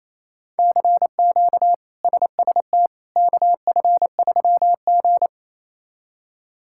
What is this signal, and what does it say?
Morse code, heard as a keyed tone, about 27 wpm, 700 Hz. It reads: CQ SST KF3G
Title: K1USN SST QSO Practice - 28wpm